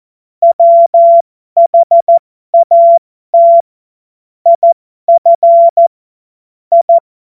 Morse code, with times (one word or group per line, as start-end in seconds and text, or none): 0.4-3.6 WHAT
4.5-5.9 IF
6.7-7.0 I